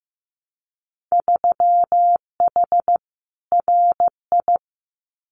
Morse code, heard as a keyed tone, about 15 wpm, 700 Hz. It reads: 3H RI
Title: Winter Field Day Exchanges Encore 15wpm